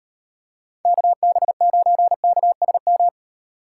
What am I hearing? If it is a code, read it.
Morse code, heard as a keyed tone, about 30 wpm, 700 Hz. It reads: KB9KSM
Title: Speed Racing - USA 2x3 Call Signs 25wpm